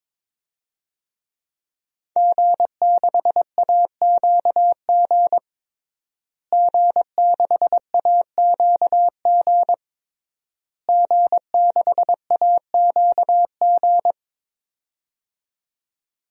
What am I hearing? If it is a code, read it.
Morse code, heard as a keyed tone, about 22 wpm, 700 Hz. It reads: G6AQG G6AQG G6AQG
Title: DXCC-100 Call Signs - 5 Characters - Repeated 3x - 22wpm